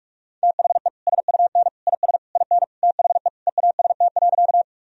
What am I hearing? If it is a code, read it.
Morse code, heard as a keyed tone, about 46 wpm, 700 Hz. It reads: THE SUN IS IN THE EAST.
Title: Sentences from Top 1000 Words - Review Part 3 - 45wpm